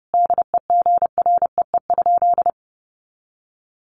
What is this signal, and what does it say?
Morse code, heard as a keyed tone, about 28 wpm, 700 Hz. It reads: DEGREE?
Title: Sentences from Top 1500 Words 30wpm